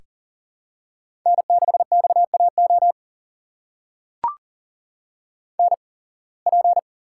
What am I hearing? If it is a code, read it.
Morse code, heard as a keyed tone, about 40 wpm, spaced as for 26 wpm, 700 Hz. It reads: N6XAO NP
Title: USA Call Signs - 5 Characters - 8x Character Spacing - 40wpm